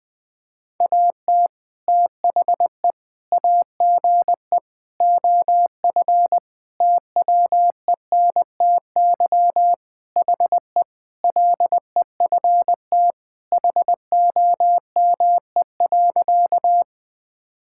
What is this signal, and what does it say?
Morse code, heard as a keyed tone, about 20 wpm, 700 Hz. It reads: AT THE AGE OF TWENTY HE LEFT HOME.
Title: Sentences with 4-Word Phrases - 20wpm